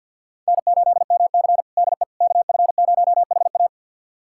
Morse code, heard as a keyed tone, about 50 wpm, 700 Hz. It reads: N8MC DE KF0HA
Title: USA Callsign DE Callsign 50wpm